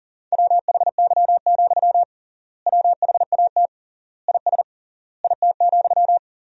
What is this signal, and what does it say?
Morse code, heard as a keyed tone, about 40 wpm, 700 Hz. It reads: WHY, WHAT IS IT,
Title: 3x Word-Spacing - Sentences from Top 200 Words - 40wpm